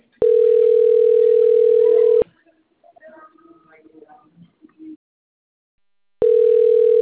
Ringback tone (North America), 440 + 480 Hz.
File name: rg-1600-unknown- - - .52374